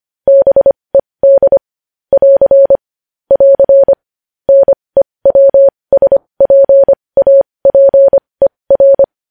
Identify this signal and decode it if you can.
Morse code, heard as a keyed tone, about 25 wpm, spaced as for 20 wpm, 550 Hz. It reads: BED <AR> <AR> NEWSPAPER
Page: news-headlines-in-morse-code-at-20-wpm